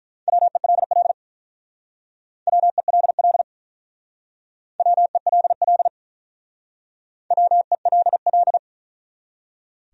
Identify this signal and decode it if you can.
Morse code, heard as a keyed tone, about 44 wpm, 700 Hz. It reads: WELL WELL WELL WELL